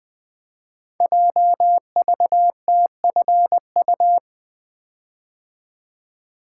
Morse code, heard as a keyed tone, about 20 wpm, 700 Hz. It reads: JVTFU
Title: Code Groups - 5 Letter:Numbers - 20wpm